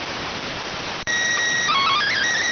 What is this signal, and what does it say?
SSTV (slow-scan television), PD 290.